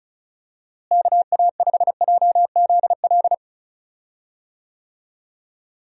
Morse code, heard as a keyed tone, about 34 wpm, 700 Hz. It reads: KA5JZL
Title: USA Call Signs - 6 Characters - Repeated 2x - 35wpm